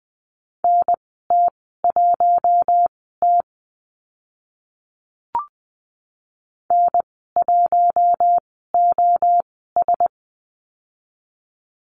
Morse code, heard as a keyed tone, about 20 wpm, spaced as for 15 wpm, 700 Hz. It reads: NT1T N1OS
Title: 042 Lesson 8g - Number 1 - Call Signs 20wpm